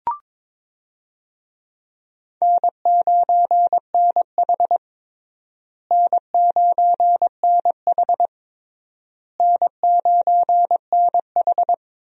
Morse code, heard as a keyed tone, about 22 wpm, 700 Hz. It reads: N9NH N9NH N9NH
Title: USA Call Signs - 4 Characters - Repeated 3x - 22wpm